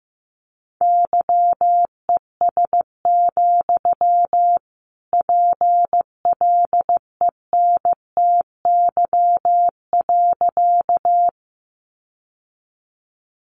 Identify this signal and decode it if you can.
Morse code, heard as a keyed tone, about 15 wpm, 700 Hz. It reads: YES, PLENTY.
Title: Sentences from Top 1300 Words 15wpm